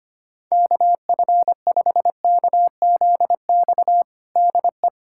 Morse code, heard as a keyed tone, about 25 wpm, 700 Hz. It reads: KF5KZX DE
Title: USA Callsign DE Callsign 25wpm